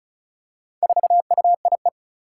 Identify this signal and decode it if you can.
Morse code, heard as a keyed tone, about 34 wpm, 700 Hz. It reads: 4UIE
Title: Four Challenging Letter-Number Combinations 35wpm